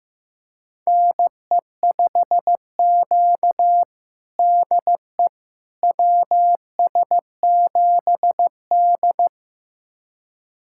Morse code, heard as a keyed tone, about 16 wpm, 700 Hz. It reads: NE5Q DE WS7D
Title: USA Callsign DE Callsign - 4 Characters - 15wpm